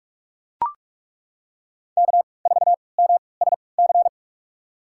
Morse code, heard as a keyed tone, about 42 wpm, spaced as for 25 wpm, 700 Hz. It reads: K4MSC
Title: USA Call Signs - 5 Characters - 3x Character Spacing - 45wpm